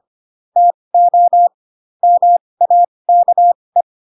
Morse code, heard as a keyed tone, about 25 wpm, spaced as for 20 wpm, 700 Hz. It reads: TO MAKE